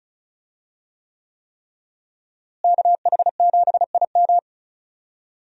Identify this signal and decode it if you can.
Morse code, heard as a keyed tone, about 36 wpm, 700 Hz. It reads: KH7IM
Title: USA Call Signs - 5 Characters - Repeated 3x - 35wpm